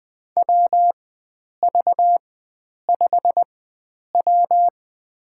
Morse code, heard as a keyed tone, about 20 wpm, spaced as for 9 wpm, 700 Hz. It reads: WV5W
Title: USA Call Signs - 4 Characters - 4x Character Spacing - 20wpm